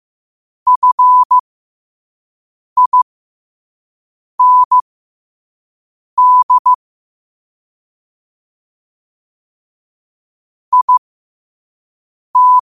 Morse code, heard as a keyed tone, about 15 wpm, spaced as for 5 wpm, 1000 Hz. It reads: FIND IT
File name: qotdfiles / 5wpm